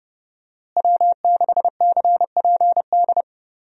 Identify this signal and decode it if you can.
Morse code, heard as a keyed tone, about 29 wpm, 700 Hz. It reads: W6CPD
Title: USA Call Signs - 5 Characters - Repeated 3x - 30wpm